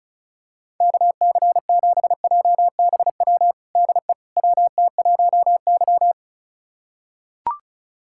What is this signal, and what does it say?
Morse code, heard as a keyed tone, about 34 wpm, 700 Hz. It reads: KC7JBW DE WT1Y
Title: USA Callsign DE Callsign 35wpm